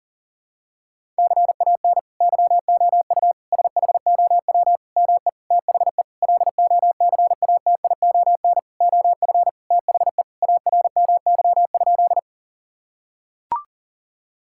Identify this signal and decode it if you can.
Morse code, heard as a keyed tone, about 37 wpm, 700 Hz. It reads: CAN YOU SHOW ME THE LOCATION OF THE ARMY?
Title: Sentences from Top 1000 Words - Review Part 6 - 40wpm